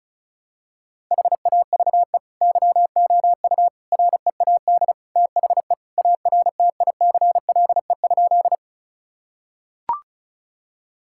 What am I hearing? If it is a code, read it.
Morse code, heard as a keyed tone, about 36 wpm, 700 Hz. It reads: HAVE YOU READ THE ARTICLE?
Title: Sentences from Top 1000 Words - Review Part 5 - 35wpm